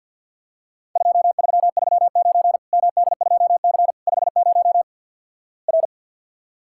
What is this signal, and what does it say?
Morse code, heard as a keyed tone, about 50 wpm, 700 Hz. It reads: 2339 MD2C 50 R
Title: Worked All Europe DX Contest - QTC Exchange - 50wpm